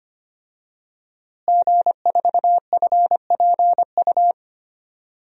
Morse code, heard as a keyed tone, about 25 wpm, 700 Hz. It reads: G4FPU
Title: DXCC-100 Call Signs - 5 Characters - Repeated 2x - 25wpm